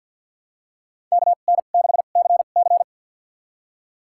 Morse code, heard as a keyed tone, about 50 wpm, spaced as for 36 wpm, 700 Hz. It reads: KN6CC
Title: USA Call Signs - 5 Characters - 2x Character Spacing - 50wpm